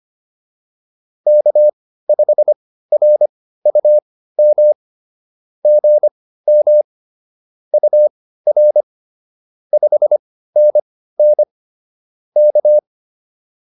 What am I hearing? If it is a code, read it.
Morse code, heard as a keyed tone, about 25 wpm, spaced as for 15 wpm, 590 Hz. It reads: K5RUM GM UR 5NN K